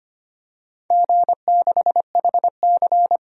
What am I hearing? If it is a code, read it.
Morse code, heard as a keyed tone, about 25 wpm, 700 Hz. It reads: G6HC
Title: DXCC-100 Call Signs - 4 Characters - 25wpm